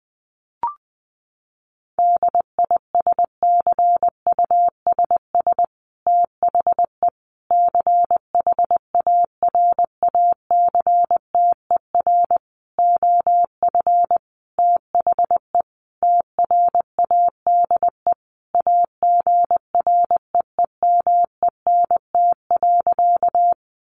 Morse code, heard as a keyed tone, about 20 wpm, 700 Hz. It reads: DISCUSS THE CHARACTER OF THE TRADE AGREEMENT.
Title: Sentences from Top 1500 Words 20wpm